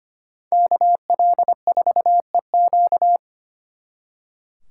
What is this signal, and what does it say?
Morse code, heard as a keyed tone, about 30 wpm, 700 Hz. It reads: KL4EQ
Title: Speed Racing - USA 2x2 Call Signs 25wpm